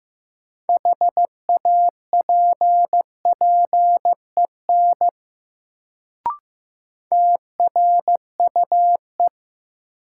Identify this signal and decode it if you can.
Morse code, heard as a keyed tone, about 15 wpm, 700 Hz. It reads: HAPPEN TRUE
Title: Sentences from Top 400 Words - New Word Drill - 15wpm